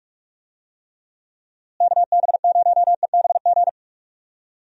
Morse code, heard as a keyed tone, about 46 wpm, 700 Hz. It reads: KB0EBG
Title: USA Call Signs - 6 Characters - Repeated 2x - 45wpm